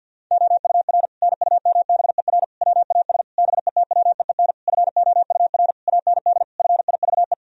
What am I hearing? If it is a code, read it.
Morse code, heard as a keyed tone, about 51 wpm, 700 Hz. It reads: OUR NUMBER WAS BETWEEN FOUR AND FIVE
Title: Sentences from Top 1000 Words - Review Part 2 - 50wpm